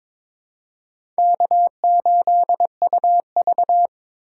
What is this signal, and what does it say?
Morse code, heard as a keyed tone, about 23 wpm, 700 Hz. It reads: K8UV
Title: USA Call Signs - 4 Characters - 3x Character Spacing - 22wpm